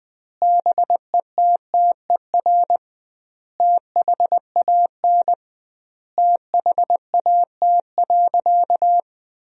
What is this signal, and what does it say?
Morse code, heard as a keyed tone, about 20 wpm, 700 Hz. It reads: BETTER THAN THAT.
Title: 2x Word-Spacing - Sentences from Top 400 Words - 20wpm